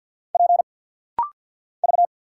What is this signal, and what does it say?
Morse code, heard as a keyed tone, about 49 wpm, 700 Hz. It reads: P V
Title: Single Letter-Number Alt - Rapid-Fire 50wpm